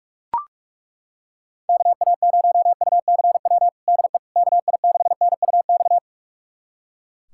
Morse code, heard as a keyed tone, about 41 wpm, 700 Hz. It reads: KA0UCW DE KI6NUX
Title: USA Callsign DE Callsign - 6 Characters - 45wpm